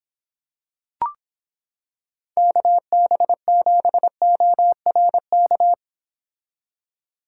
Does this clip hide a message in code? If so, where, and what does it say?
Yes: Morse code from 2.4 to 5.7 s, heard as a keyed tone, about 26 wpm, 700 Hz. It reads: KB7ORK